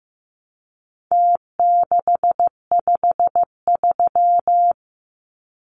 Morse code, heard as a keyed tone, about 15 wpm, 700 Hz. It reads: T653
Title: Numbers - Contest Style - 4-Digit 15wpm